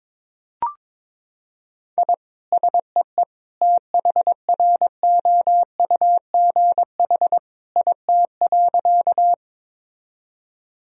Morse code, heard as a keyed tone, about 21 wpm, 700 Hz. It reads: I SEE THROUGH IT.
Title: Sentences from Top 1000 Words - Review Part 1 - 22wpm